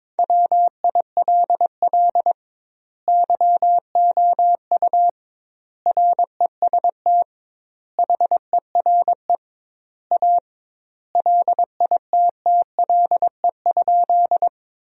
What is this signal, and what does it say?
Morse code, heard as a keyed tone, about 22 wpm, 700 Hz. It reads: WILL YOU REST HERE A LITTLE?
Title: 2x Word-Spacing - Sentences from Top 500 Words - 22wpm